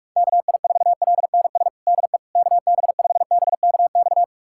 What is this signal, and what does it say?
Morse code, heard as a keyed tone, about 46 wpm, 700 Hz. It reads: KI4LNS DE KB5BKX
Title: USA Callsign DE Callsign 45wpm